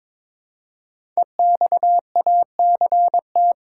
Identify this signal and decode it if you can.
Morse code, heard as a keyed tone, about 22 wpm, 690 Hz. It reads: EXACT
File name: 5 Letter Words - First 3 Letters Given - 22wpm